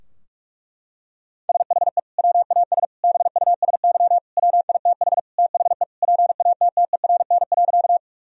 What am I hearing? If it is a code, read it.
Morse code, heard as a keyed tone, about 41 wpm, 700 Hz. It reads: SHE WAS BUSY WITH THE PATTERN.
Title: Sentences from Top 1000 Words - Review Part 2 - 45wpm